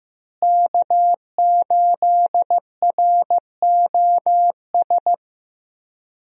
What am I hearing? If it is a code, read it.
Morse code, heard as a keyed tone, about 15 wpm, 700 Hz. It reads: K8ROS
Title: USA Call Signs - 5 Characters - Repeated 3x - 15wpm